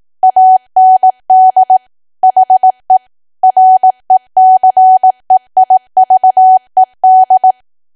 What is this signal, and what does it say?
Morse code, heard as a keyed tone, about 18 wpm, 750 Hz. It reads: AND HE RECEIVED
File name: Archive / 18 WPM / 18WPM